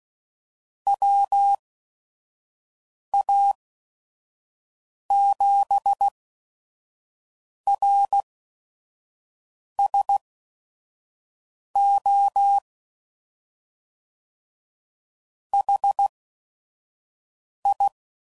Morse code, heard as a keyed tone, about 16 wpm, spaced as for 5 wpm, 800 Hz. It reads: WA7RSO HI
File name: CW Code Groups @5WPM / QSOs / QSO WD0FBT